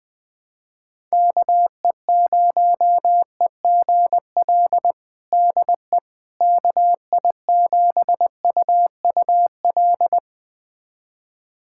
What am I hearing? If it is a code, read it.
Morse code, heard as a keyed tone, about 21 wpm, 700 Hz. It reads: KE0EGL DE KI7UUL
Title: USA Callsign DE Callsign 20wpm